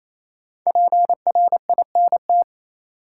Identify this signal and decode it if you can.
Morse code, heard as a keyed tone, about 28 wpm, 700 Hz. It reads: PRINT
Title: Sentences from Top 1400 Words - New Word Drill Alt - 28wpm